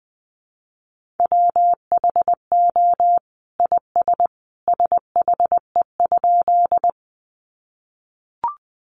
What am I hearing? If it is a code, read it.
Morse code, heard as a keyed tone, about 20 wpm, 700 Hz. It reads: WHO IS SHE?
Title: Sentences from Top 100 Words 20wpm